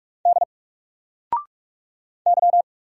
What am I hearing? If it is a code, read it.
Morse code, heard as a keyed tone, about 44 wpm, 700 Hz. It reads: D Y
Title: Single Character (Letter:Number:Punctuation:Prosign) 45wpm